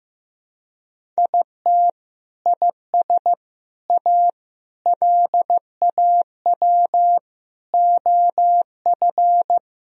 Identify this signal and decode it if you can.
Morse code, heard as a keyed tone, about 15 wpm, 700 Hz. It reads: IT IS A LAW OF
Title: Sentences from Top 1000 Words - Review Part 3 - 15wpm